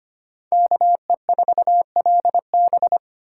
Morse code, heard as a keyed tone, about 33 wpm, 700 Hz. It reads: KE4LB